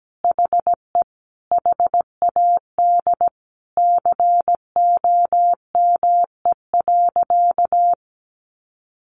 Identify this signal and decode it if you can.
Morse code, heard as a keyed tone, about 17 wpm, 700 Hz. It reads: HE HAD COME.
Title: Sentences from Top 1100 Words 17wpm